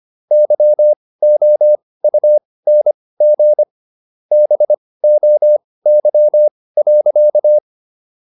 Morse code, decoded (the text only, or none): YOUNG BOY.